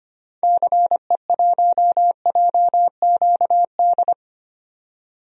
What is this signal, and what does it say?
Morse code, heard as a keyed tone, about 25 wpm, 700 Hz. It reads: CE1JQD
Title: DXCC-100 Call Signs - 6 Characters - 25wpm